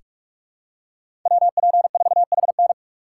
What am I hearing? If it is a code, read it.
Morse code, heard as a keyed tone, about 46 wpm, 700 Hz. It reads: WP4HN
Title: USA Call Signs - 5 Characters - 3x Character Spacing - 45wpm